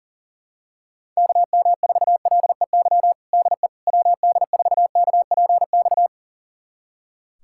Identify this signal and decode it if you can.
Morse code, heard as a keyed tone, about 40 wpm, 700 Hz. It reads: KM4LEY DE WD4KPX
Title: USA Callsign DE Callsign 40wpm